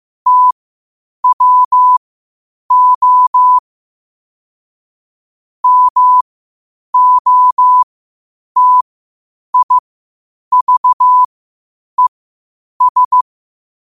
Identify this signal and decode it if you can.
Morse code, heard as a keyed tone, about 15 wpm, spaced as for 8 wpm, 1000 Hz. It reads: TWO MOTIVES
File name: qotdfiles / 8wpm